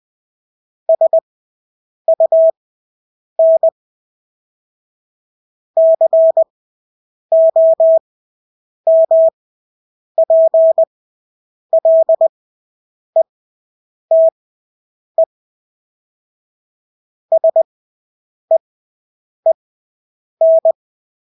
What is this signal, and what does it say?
Morse code, heard as a keyed tone, about 20 wpm, spaced as for 8 wpm, 650 Hz. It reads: SUN COMPLETE SEEN
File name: Engelske ord 20/8 wpm